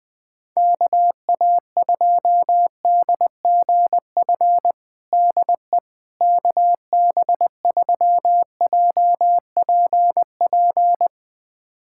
Morse code, heard as a keyed tone, about 20 wpm, 700 Hz. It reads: KA2DGF DE KB3JPP
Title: USA Callsign DE Callsign - 6 Characters - 20wpm